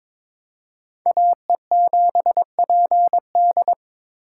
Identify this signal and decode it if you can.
Morse code, heard as a keyed tone, about 26 wpm, 700 Hz. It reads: AE7PD
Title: Speed Racing - USA 2x2 Call Signs 22wpm